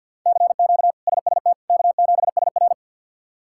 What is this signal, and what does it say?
Morse code, heard as a keyed tone, about 50 wpm, 700 Hz. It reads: CQ SST K7SR
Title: K1USN SST QSO Practice - 50wpm